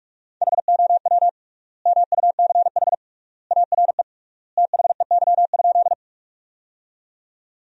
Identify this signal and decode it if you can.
Morse code, heard as a keyed tone, about 44 wpm, 700 Hz. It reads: HOW MUCH ARE THEY?
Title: 3x Word-Spacing - Sentences from Top 200 Words - 45wpm